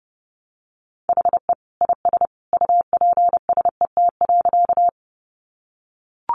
Morse code, heard as a keyed tone, about 30 wpm, 700 Hz. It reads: HE IS UPSET.
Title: Additional Easy Phrases 30wpm